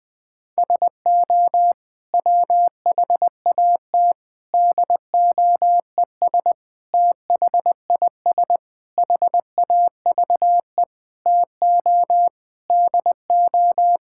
Morse code, heard as a keyed tone, about 20 wpm, 700 Hz. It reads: SO WHAT DOES THIS HAVE TO DO
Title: Sentences with 8-Word Phrases - Spoken Beginning - 20wpm